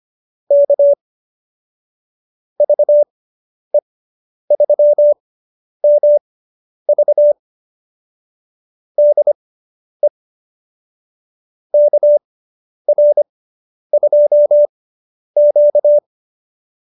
Morse code, heard as a keyed tone, about 25 wpm, spaced as for 10 wpm, 590 Hz. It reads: K VE3MV DE KR2Q